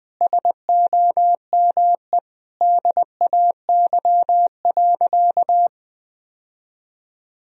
Morse code, heard as a keyed tone, about 20 wpm, 700 Hz. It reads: SOME DAY.
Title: Sentences from Top 1000 Words - Review Part 4 - 20wpm